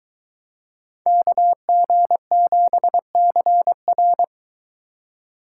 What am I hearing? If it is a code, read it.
Morse code, heard as a keyed tone, about 22 wpm, 700 Hz. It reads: KG7CR